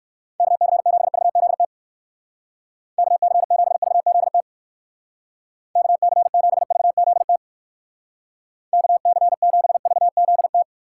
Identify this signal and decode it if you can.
Morse code, heard as a keyed tone, about 57 wpm, 700 Hz. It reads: KC7VZT KC7VZT KC7VZT KC7VZT